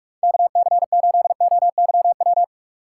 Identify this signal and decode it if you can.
Morse code, heard as a keyed tone, about 57 wpm, 700 Hz. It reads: KC8OYW